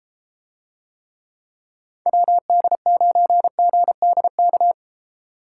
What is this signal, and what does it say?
Morse code, heard as a keyed tone, about 32 wpm, 700 Hz. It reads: WD9GDK